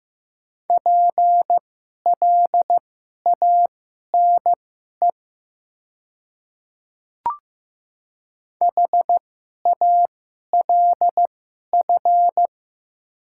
Morse code, heard as a keyed tone, about 15 wpm, spaced as for 11 wpm, 700 Hz. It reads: PLANE HALF
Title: 2x Character Spacing - Top 500 Words - 15wpm